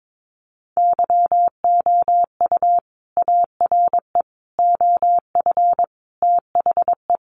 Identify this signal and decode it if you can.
Morse code, heard as a keyed tone, about 22 wpm, 700 Hz. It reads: YOU ARE OF THE
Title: Sentences from Top 1400 Words 22wpm